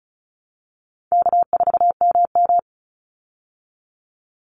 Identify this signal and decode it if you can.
Morse code, heard as a keyed tone, about 34 wpm, 700 Hz. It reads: K4MM
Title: Call Signs Repeated 3x - 35wpm